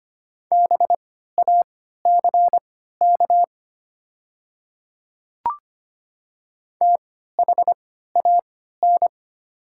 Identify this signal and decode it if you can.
Morse code, heard as a keyed tone, about 25 wpm, spaced as for 14 wpm, 700 Hz. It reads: BACK THAN
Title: 3x Character Spacing - Top 300 Words - 25wpm